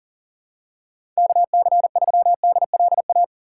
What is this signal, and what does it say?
Morse code, heard as a keyed tone, about 40 wpm, 690 Hz. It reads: KC3DLA
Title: USA Call Signs - 6 Characters - 2x Character Spacing - 40wpm